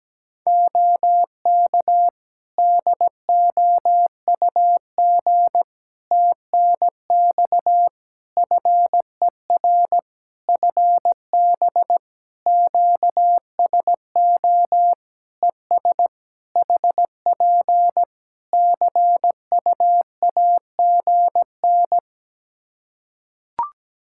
Morse code, heard as a keyed tone, about 17 wpm, 700 Hz. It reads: OK DOUG TNX FER FB QSO ES HP CUAGN
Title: Practice QSO - Exchange 4 - Ending - 17wpm